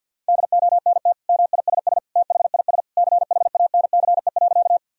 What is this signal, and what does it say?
Morse code, heard as a keyed tone, about 50 wpm, 700 Hz. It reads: DONT MISS THIS CHANCE.
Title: Sentences from Top 1000 Words - Review Part 3 - 50wpm